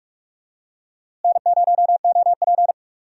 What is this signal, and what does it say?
Morse code, heard as a keyed tone, about 43 wpm, 700 Hz. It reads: N0OP
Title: USA Call Signs - 4 Characters - Repeated 2x - 45wpm